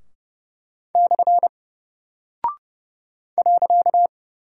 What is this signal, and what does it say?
Morse code, heard as a keyed tone, about 30 wpm, 700 Hz. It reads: / .
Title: Single Character (Letter:Number:Punctuation:Prosign) Alt 30wpm